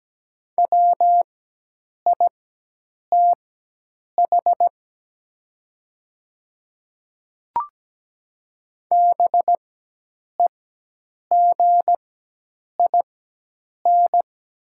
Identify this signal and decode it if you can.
Morse code, heard as a keyed tone, about 17 wpm, spaced as for 8 wpm, 700 Hz. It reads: WITH BEGIN
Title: 4x Character Spacing - Top 300 Words - 17wpm